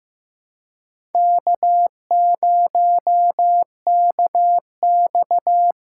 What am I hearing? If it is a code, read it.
Morse code, heard as a keyed tone, about 16 wpm, 700 Hz. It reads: K0KX
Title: K1USN SST QSO Practice - 15wpm